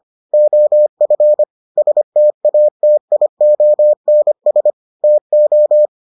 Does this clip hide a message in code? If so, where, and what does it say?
Yes: Morse code from 0.3 to 5.9 s, heard as a keyed tone, about 25 wpm, 600 Hz. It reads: OF STATIONS TO